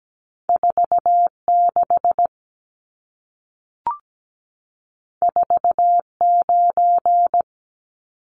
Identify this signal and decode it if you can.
Morse code, heard as a keyed tone, about 17 wpm, 700 Hz. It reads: 46 49
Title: Numbers - 2-Digit 17wpm